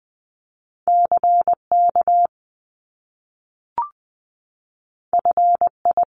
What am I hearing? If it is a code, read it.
Morse code, heard as a keyed tone, about 20 wpm, 700 Hz. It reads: CK FI
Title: Top Two Letter Combinations in Words 20wpm